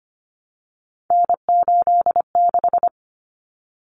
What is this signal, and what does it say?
Morse code, heard as a keyed tone, about 25 wpm, 700 Hz. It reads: N86
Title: Common Cut Numbers - 3-Digit 25wpm